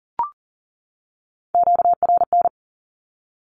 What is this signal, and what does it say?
Morse code, heard as a keyed tone, about 40 wpm, 700 Hz. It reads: QRN
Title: Q-Codes 40wpm